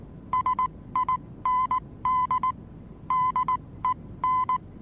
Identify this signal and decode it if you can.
Morse code, heard as a keyed tone, about 19 wpm, spaced as for 16 wpm, 1020 Hz. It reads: SIND DEN
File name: repos / Morsekurs / DJ4UF Morsen26